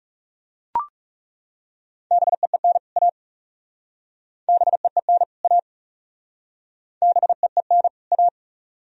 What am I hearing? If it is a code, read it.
Morse code, heard as a keyed tone, about 40 wpm, 700 Hz. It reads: BEEN A BEEN A BEEN A